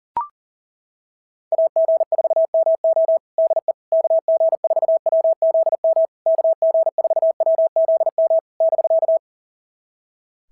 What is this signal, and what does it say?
Morse code, heard as a keyed tone, about 37 wpm, 650 Hz. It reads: AG4MO DE KG4WZM KG4WZM <BK>
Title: SOTA QSO Practice 40wpm